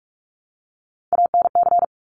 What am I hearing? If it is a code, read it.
Morse code, heard as a keyed tone, about 46 wpm, 700 Hz. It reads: ANC
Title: Top Three Letter Combinations in Words 45wpm